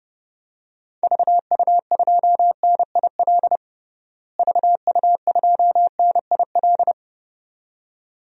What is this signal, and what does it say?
Morse code, heard as a keyed tone, about 30 wpm, 700 Hz. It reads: VU2NIL VU2NIL
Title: DXCC-100 Call Signs - 6 Characters - Repeated 2x - 30wpm